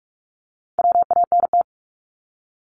Morse code, heard as a keyed tone, about 46 wpm, 690 Hz. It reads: WANT
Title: Multi-letter Phonemes 45wpm